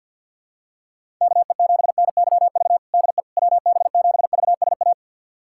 Morse code, heard as a keyed tone, about 52 wpm, 700 Hz. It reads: KE7NYV DE WB7VSA
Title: USA Callsign DE Callsign 50wpm